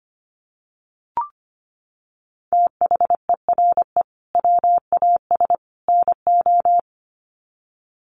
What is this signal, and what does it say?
Morse code, heard as a keyed tone, about 25 wpm, 700 Hz. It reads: THERE WAS NO
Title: Sets of 3 Words 25wpm